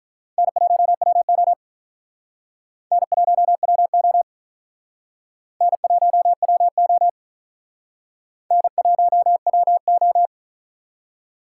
Morse code, heard as a keyed tone, about 43 wpm, 700 Hz. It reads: N1WO N1WO N1WO N1WO
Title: Speed Racing - USA 1x2 Call Signs 35wpm